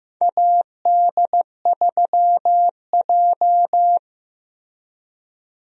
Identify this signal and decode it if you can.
Morse code, heard as a keyed tone, about 15 wpm, 700 Hz. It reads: AD3J
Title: USA Call Signs - 4 Characters - Repeated 3x - 15wpm